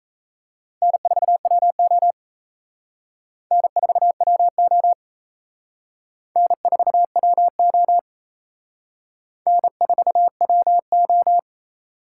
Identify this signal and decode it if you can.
Morse code, heard as a keyed tone, about 35 wpm, 700 Hz. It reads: N4WO N4WO N4WO N4WO